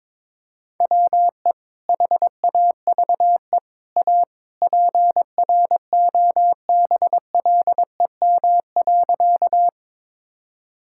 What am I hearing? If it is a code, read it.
Morse code, heard as a keyed tone, about 22 wpm, 700 Hz. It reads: WE HAVE A PROBLEM.
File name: practice / 22wpm / Sentences with 3-Word Phrases - 22wpm